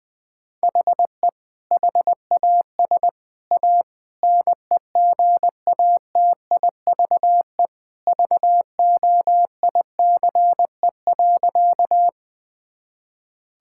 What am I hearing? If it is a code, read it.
Morse code, heard as a keyed tone, about 21 wpm, 700 Hz. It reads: HE HAS A NEGATIVE VOICE.
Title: Sentences from Top 1000 Words - Review Part 6 - 20wpm